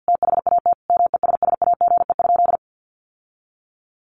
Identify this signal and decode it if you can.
Morse code, heard as a keyed tone, about 50 wpm, 700 Hz. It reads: THAT MESSAGE?
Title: Sentences from Top 1100 Words 50wpm